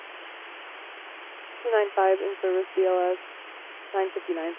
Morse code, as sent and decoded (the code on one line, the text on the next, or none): none